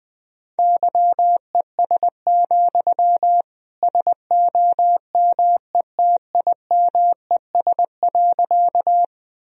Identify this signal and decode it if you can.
Morse code, heard as a keyed tone, about 20 wpm, 700 Hz. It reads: YES, SOMETIMES.
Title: Sentences from Top 1000 Words - Review Part 3 - 20wpm